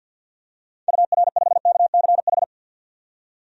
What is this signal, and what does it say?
Morse code, heard as a keyed tone, about 52 wpm, 700 Hz. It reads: UR5KCH
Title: DXCC-100 Call Signs - 6 Characters - 50wpm